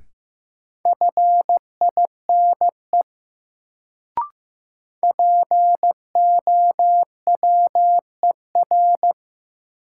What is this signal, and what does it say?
Morse code, heard as a keyed tone, about 15 wpm, 700 Hz. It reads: FINE POWER
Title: Sentences from Top 500 Words - New Word Drill - 15wpm